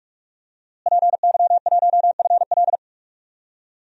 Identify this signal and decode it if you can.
Morse code, heard as a keyed tone, about 43 wpm, 700 Hz. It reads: PY1FL
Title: DXCC-100 Call Signs - 5 Characters - Repeated 2x - 45wpm